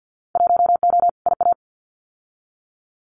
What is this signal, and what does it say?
Morse code, heard as a keyed tone, about 50 wpm, 700 Hz. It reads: 1O IA
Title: Winter Field Day Exchanges Encore 50wpm